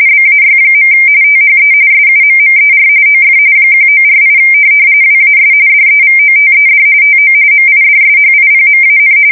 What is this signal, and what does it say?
RTTY, about 45.45 baud. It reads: BAY. THE SUMMER OF 1948 WAS DEVOTED TO MIDSHIPMAN
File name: fsk170 / USS-Missouri